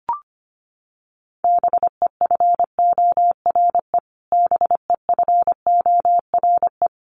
Morse code, heard as a keyed tone, about 25 wpm, 700 Hz. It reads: BEFORE BEFORE
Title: Top 500 Words Repeated 25wpm